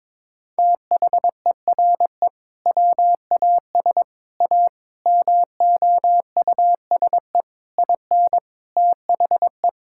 Morse code, heard as a keyed tone, about 21 wpm, 700 Hz. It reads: THERE WAS A MOUSE IN THE
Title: Sentences from Top 1000 Words - Review Part 5 - 22wpm